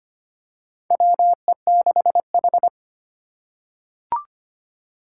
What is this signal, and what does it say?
Morse code, heard as a keyed tone, about 25 wpm, 700 Hz. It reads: WE6H
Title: USA Call Signs - 4 Characters - 3x Character Spacing - 25wpm